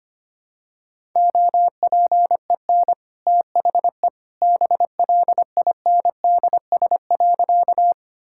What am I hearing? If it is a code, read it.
Morse code, heard as a keyed tone, about 25 wpm, 700 Hz. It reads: OPEN THE BLINDS.
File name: Easy Sentences with Hints - 25wpm